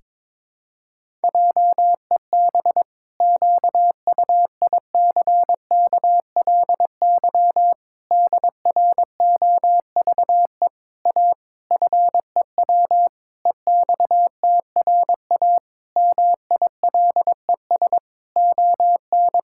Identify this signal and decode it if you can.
Morse code, heard as a keyed tone, about 22 wpm, 700 Hz. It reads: JEB QUICKLY DROVE A FEW EXTRA MILES ON
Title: Pangram Challenge v2 - 22wpm